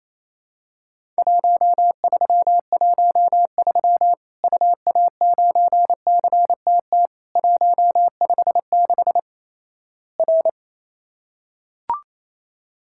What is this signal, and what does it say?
Morse code, heard as a keyed tone, about 27 wpm, 700 Hz. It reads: 1313 UA9CTT 156 R
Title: Worked All Europe DX Contest - QTC Exchange - 28wpm